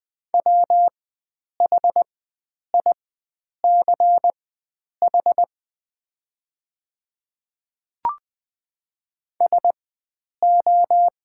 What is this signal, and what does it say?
Morse code, heard as a keyed tone, about 20 wpm, spaced as for 9 wpm, 690 Hz. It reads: WHICH SO
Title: 4x Character Spacing - Top 100 Words - 20wpm